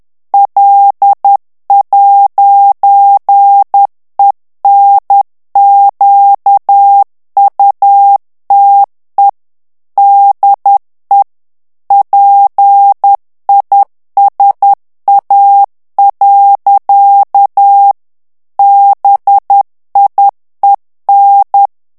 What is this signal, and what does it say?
Morse code, heard as a keyed tone, about 10 wpm, 800 Hz. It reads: L'ENQUTE DE PISA. BIEN
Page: Français - texte en clair 1 - vitesse : 10 mots minute